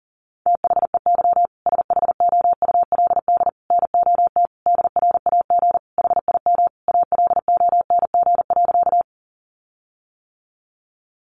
Morse code, heard as a keyed tone, about 40 wpm, 700 Hz. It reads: THEY SHOULD NOT DRAG HIM ALONG.
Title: Sentences from Top 1500 Words 40wpm